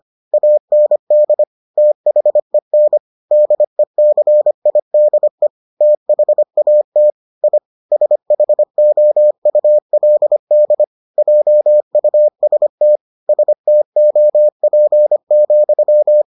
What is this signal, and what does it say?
Morse code, heard as a keyed tone, about 25 wpm, 600 Hz. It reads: AND THEN DECIDE THAT I SHOULD JUST STOP,